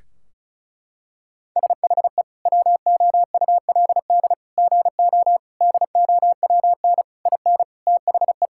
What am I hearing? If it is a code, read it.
Morse code, heard as a keyed tone, about 36 wpm, 700 Hz. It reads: SHE WOULD GO DOWN IN THE
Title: Sentences from Top 1600 Words 35wpm